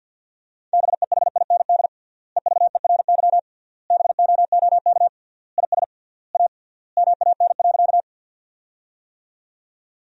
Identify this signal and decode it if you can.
Morse code, heard as a keyed tone, about 48 wpm, 700 Hz. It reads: BEHIND EVERY BOOK IS A MAN.
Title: 3x Word-Spacing - Sentences from Top 500 Words - 50wpm